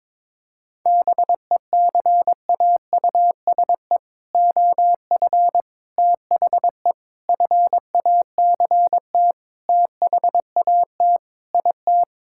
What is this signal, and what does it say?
Morse code, heard as a keyed tone, about 22 wpm, 700 Hz. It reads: BECAUSE OF THE FACT THAT IT
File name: practice / 22wpm / Sentences with 4-Word Phrases - 22wpm